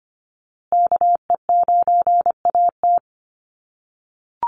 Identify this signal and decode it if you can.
Morse code, heard as a keyed tone, about 25 wpm, 700 Hz. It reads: KE9AT
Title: USA Call Signs Repeated 2x 25wpm